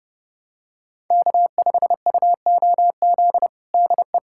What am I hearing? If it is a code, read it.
Morse code, heard as a keyed tone, about 32 wpm, 700 Hz. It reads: K5UOZ DE
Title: USA Callsign DE Callsign 30wpm